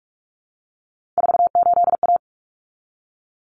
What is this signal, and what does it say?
Morse code, heard as a keyed tone, about 44 wpm, 700 Hz. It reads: 48A
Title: Common Cut Numbers - 3-Digit 45wpm